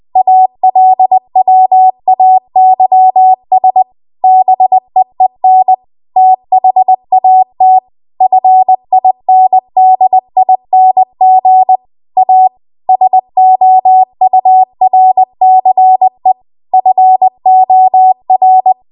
Morse code, heard as a keyed tone, about 20 wpm, 750 Hz. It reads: ALWAYS BEEN THAT FINDING A SOURCE FOR
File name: Archive / 20 WPM / 20WPM